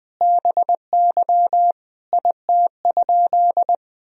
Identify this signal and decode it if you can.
Morse code, heard as a keyed tone, about 20 wpm, 700 Hz. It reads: BY IT?
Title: Sentences from Top 100 Words and Abbreviations 20wpm